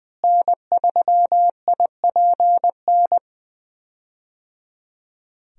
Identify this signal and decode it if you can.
Morse code, heard as a keyed tone, about 20 wpm, 700 Hz. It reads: N3IPN
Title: USA Call Signs - 5 Characters - Repeated 3x - 20wpm